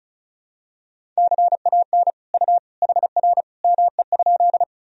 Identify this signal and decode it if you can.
Morse code, heard as a keyed tone, about 34 wpm, 700 Hz. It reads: CAN U HR ME?
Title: Sentences from Top 300 Words and Abbreviations 35wpm